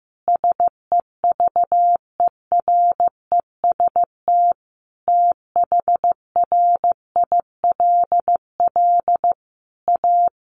Morse code, heard as a keyed tone, about 15 wpm, 700 Hz. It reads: SEVEREST THRILL A
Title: A Sound Of Thunder 15wpm